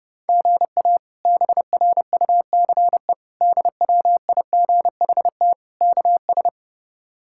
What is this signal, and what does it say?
Morse code, heard as a keyed tone, about 30 wpm, 700 Hz. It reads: GA BRUCE DWIGHT KS
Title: K1USN SST QSO Practice - 30wpm